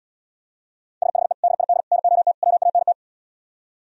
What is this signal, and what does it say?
Morse code, heard as a keyed tone, about 77 wpm, 700 Hz. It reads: SHE DID NOT WANT